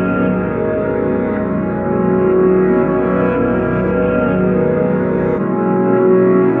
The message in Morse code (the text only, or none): none